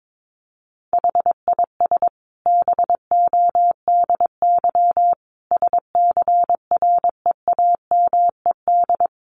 Morse code, heard as a keyed tone, about 22 wpm, 700 Hz. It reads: HIS BODY SCREAMED
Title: A Sound Of Thunder 22wpm